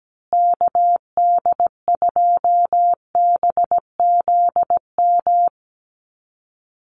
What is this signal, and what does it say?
Morse code, heard as a keyed tone, about 17 wpm, 700 Hz. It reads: KD2BZM
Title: USA Call Signs Repeated 2x 17wpm